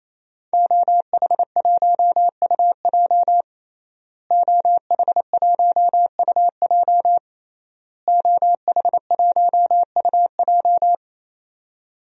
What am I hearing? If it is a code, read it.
Morse code, heard as a keyed tone, about 28 wpm, 700 Hz. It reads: OH1UJ OH1UJ OH1UJ
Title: DXCC-100 Call Signs - 5 Characters - Repeated 3x - 28wpm